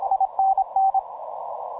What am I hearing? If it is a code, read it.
Morse code, heard as a keyed tone, about 26 wpm, 770 Hz, with radio noise. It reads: INN
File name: simple-file-list / E51CIK 15MCW